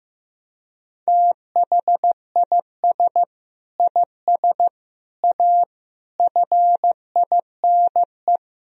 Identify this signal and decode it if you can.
Morse code, heard as a keyed tone, about 15 wpm, 700 Hz. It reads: THIS IS A FINE
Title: Sentences from Top 1000 Words - Review Part 1 - 15wpm